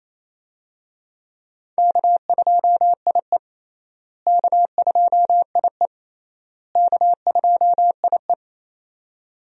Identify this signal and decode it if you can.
Morse code, heard as a keyed tone, about 28 wpm, 700 Hz. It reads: K2IE K2IE K2IE
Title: USA Call Signs - 4 Characters - Repeated 3x - 28wpm